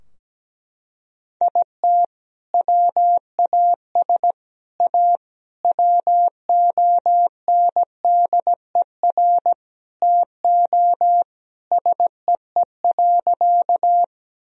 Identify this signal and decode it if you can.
Morse code, heard as a keyed tone, about 17 wpm, 700 Hz. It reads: IT WAS A WONDER TO SEE.
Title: Sentences from Top 1000 Words - Review Part 3 - 17wpm